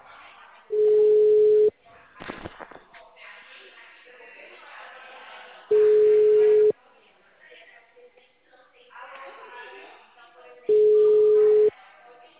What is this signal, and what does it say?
Ringback tone (Europe, 425 Hz), with speech.
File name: out- -2046- - - .5277